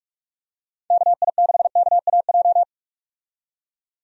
Morse code, heard as a keyed tone, about 45 wpm, 700 Hz. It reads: KI6KAJ
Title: USA Call Signs - 6 Characters - 45wpm